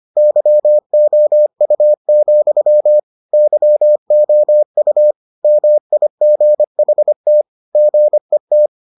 Morse code, heard as a keyed tone, about 25 wpm, 600 Hz. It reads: YOU, YOU MIGHT GET